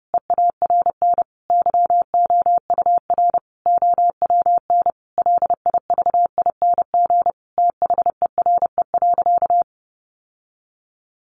Morse code, heard as a keyed tone, about 30 wpm, 700 Hz. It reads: EARN YOUR OWN LIVING THERE.
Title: Sentences from Top 1500 Words 30wpm